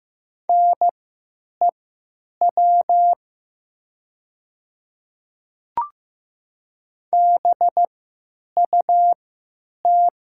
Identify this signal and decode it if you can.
Morse code, heard as a keyed tone, about 15 wpm, spaced as for 9 wpm, 700 Hz. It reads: NEW BUT
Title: 3x Character Spacing - Top 300 Words - 15wpm